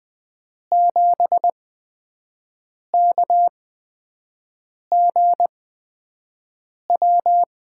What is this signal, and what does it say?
Morse code, heard as a keyed tone, about 20 wpm, spaced as for 5 wpm, 700 Hz. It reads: 7KGW
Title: USA Call Signs - 8x Character Spacing - 20wpm